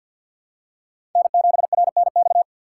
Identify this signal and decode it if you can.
Morse code, heard as a keyed tone, about 47 wpm, 700 Hz. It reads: N7RNX
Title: USA Call Signs - 6x Character Spacing - 50wpm